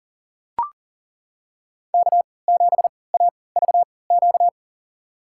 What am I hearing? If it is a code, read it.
Morse code, heard as a keyed tone, about 40 wpm, spaced as for 23 wpm, 700 Hz. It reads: K7AVQ
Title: USA Call Signs - 5 Characters - 3x Character Spacing - 40wpm